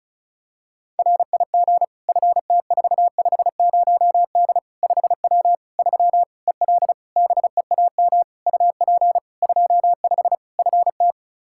Morse code, heard as a keyed tone, about 36 wpm, 700 Hz. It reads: RIG FT450D 5W 3 EL BEAM UP 25 FT
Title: Practice QSO - Exchange 2 - Info - 35wpm